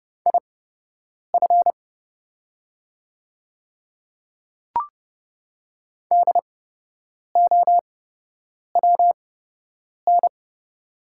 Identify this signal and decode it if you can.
Morse code, heard as a keyed tone, about 30 wpm, spaced as for 8 wpm, 700 Hz. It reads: IF DOWN
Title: 8x Character Spacing - Top 100 Words - 30wpm